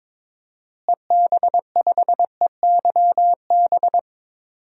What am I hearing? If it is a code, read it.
Morse code, heard as a keyed tone, about 21 wpm, 700 Hz. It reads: EB5EYB
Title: DXCC-100 Call Signs - 6 Characters - Repeated 2x - 22wpm